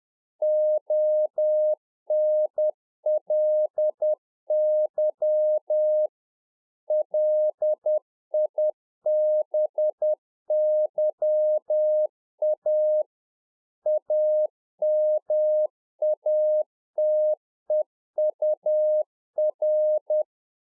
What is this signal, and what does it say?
Morse code, heard as a keyed tone, about 10 wpm, 610 Hz. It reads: ONLY LIBYA AMATEUR
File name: UNFOGETTABLE QSO 10WPM